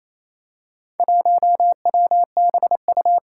Morse code, heard as a keyed tone, about 27 wpm, 700 Hz. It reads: 1WBU
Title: Four Challenging Letter-Number Combinations 28wpm